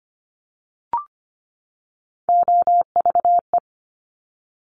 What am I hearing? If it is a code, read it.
Morse code, heard as a keyed tone, about 25 wpm, 700 Hz. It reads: OVE
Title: Top Three Letter Combinations in Words 25wpm